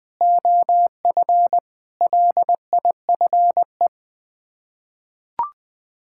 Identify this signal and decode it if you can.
Morse code, heard as a keyed tone, about 20 wpm, 700 Hz. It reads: OF LIFE
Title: Sets of 6 Words - 20wpm